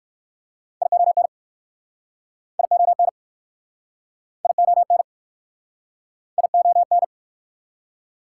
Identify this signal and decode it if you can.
Morse code, heard as a keyed tone, about 59 wpm, 690 Hz. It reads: ION ION ION ION